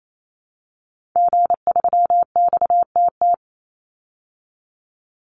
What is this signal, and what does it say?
Morse code, heard as a keyed tone, about 28 wpm, 700 Hz. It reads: G3XTT
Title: Call Signs Repeated 2x - 28wpm